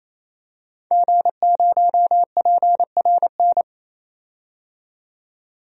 Morse code, heard as a keyed tone, about 28 wpm, 700 Hz. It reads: G0PRN
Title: DXCC-100 Call Signs - 5 Characters - 28wpm